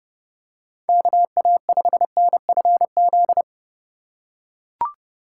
Morse code, heard as a keyed tone, about 28 wpm, 700 Hz. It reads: KA5NFZ